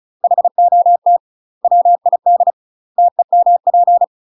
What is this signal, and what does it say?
Morse code, heard as a keyed tone, about 36 wpm, 700 Hz. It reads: HOT WID TEMP
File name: UK MASTER QSOs TRAINING / 35WPM / UK MASTER QSOs 0026